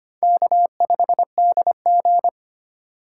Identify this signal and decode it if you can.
Morse code, heard as a keyed tone, about 25 wpm, 700 Hz. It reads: K5DG
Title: USA Call Signs - 2x Character Spacing - 25wpm